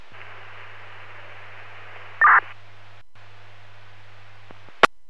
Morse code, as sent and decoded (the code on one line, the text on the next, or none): none